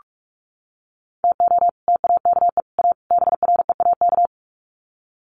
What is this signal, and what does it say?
Morse code, heard as a keyed tone, about 44 wpm, 700 Hz. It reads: TO TAKE A BREAK
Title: Sets of 4 Words 45wpm